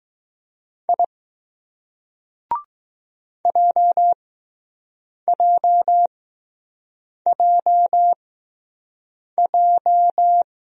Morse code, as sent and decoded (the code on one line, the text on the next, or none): .. / .--- .--- .--- .---
I JJJJ